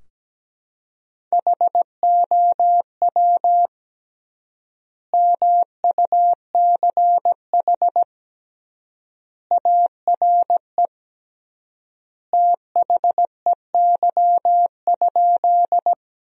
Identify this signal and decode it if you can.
Morse code, heard as a keyed tone, about 17 wpm, 700 Hz. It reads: HOW MUCH ARE THEY?
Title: 3x Word-Spacing - Sentences from Top 200 Words - 17wpm